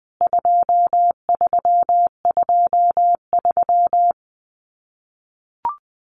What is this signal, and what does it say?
Morse code, heard as a keyed tone, about 20 wpm, 700 Hz. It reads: 2323
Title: CWT Exchange 20wpm